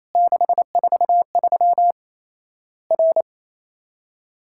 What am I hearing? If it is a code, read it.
Morse code, heard as a keyed tone, about 28 wpm, 700 Hz. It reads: 643 R
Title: Worked All Europe DX Contest - QTC Exchange - 28wpm